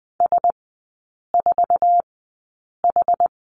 Morse code, heard as a keyed tone, about 20 wpm, spaced as for 8 wpm, 700 Hz. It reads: S4H
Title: 184 Lesson 30e - Letter b - Three Characters 20wpm